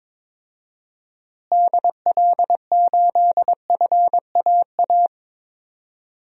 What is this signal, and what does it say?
Morse code, heard as a keyed tone, about 22 wpm, 700 Hz. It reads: DL8FAA
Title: DXCC-100 Call Signs - 6 Characters - Repeated 2x - 22wpm